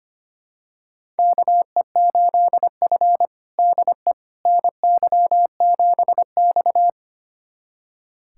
Morse code, heard as a keyed tone, about 25 wpm, 700 Hz. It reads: KE8F DE NY7X
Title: USA Callsign DE Callsign - 4 Characters - 25wpm